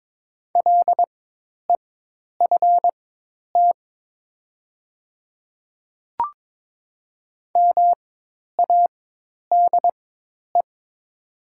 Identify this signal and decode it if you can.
Morse code, heard as a keyed tone, about 22 wpm, spaced as for 10 wpm, 700 Hz. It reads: LEFT MADE
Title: 4x Character Spacing - Top 300 Words - 22wpm